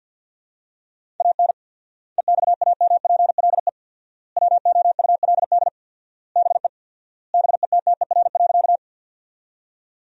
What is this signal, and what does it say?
Morse code, heard as a keyed tone, about 50 wpm, 700 Hz. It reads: AN EXAMPLE WOULD BE BETTER.
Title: 4x Word-Spacing - Sentences from Top 400 Words - 50wpm